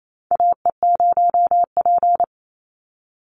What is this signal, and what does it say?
Morse code, heard as a keyed tone, about 28 wpm, 700 Hz. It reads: AE0P
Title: Call Signs - 28wpm